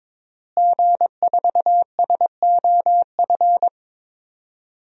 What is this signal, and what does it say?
Morse code, heard as a keyed tone, about 22 wpm, 700 Hz. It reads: G4SOF
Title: DXCC-100 Call Signs - 5 Characters - Repeated 2x - 22wpm